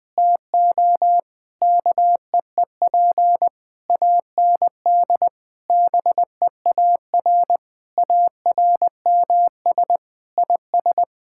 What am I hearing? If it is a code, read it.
Morse code, heard as a keyed tone, about 20 wpm, 700 Hz. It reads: TO KEEP AND BEAR ARMS IS
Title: Sentences with 9-Word Phrases - 20wpm